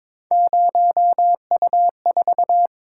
Morse code, heard as a keyed tone, about 26 wpm, 700 Hz. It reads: 0U4
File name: Speed-Racing - Three Letter-Number Combinations - 22wpm